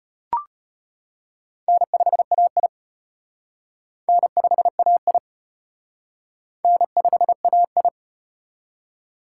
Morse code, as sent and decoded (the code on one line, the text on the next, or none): -. ..... .- .. / -. ..... .- .. / -. ..... .- ..
N5AI N5AI N5AI